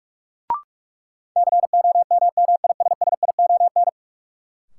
Morse code, heard as a keyed tone, about 44 wpm, 700 Hz. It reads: COMMISSION
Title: Sentences from Top 1400 Words - New Word Drill - 45wpm